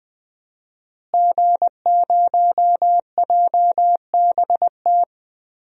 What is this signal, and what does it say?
Morse code, heard as a keyed tone, about 21 wpm, 700 Hz. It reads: G0JBT
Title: DXCC-100 Call Signs - 5 Characters - Repeated 3x - 20wpm